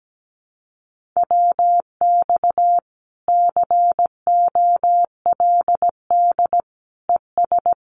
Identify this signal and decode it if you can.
Morse code, heard as a keyed tone, about 17 wpm, 700 Hz. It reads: WX COLD ES
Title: Common QSO Phrases Encore 17wpm